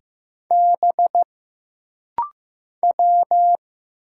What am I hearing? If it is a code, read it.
Morse code, heard as a keyed tone, about 15 wpm, 700 Hz. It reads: B W
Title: Single Letters - Rapid-Fire 15wpm